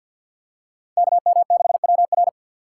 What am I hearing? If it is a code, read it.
Morse code, heard as a keyed tone, about 49 wpm, 700 Hz. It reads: KM6WR
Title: USA Call Signs - 3x Character Spacing - 50wpm